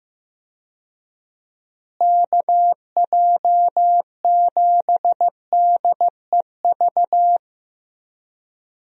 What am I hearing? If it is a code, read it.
Morse code, heard as a keyed tone, about 15 wpm, 700 Hz. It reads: KJ7DEV
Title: USA Call Signs - 6 Characters - Repeated 2x - 15wpm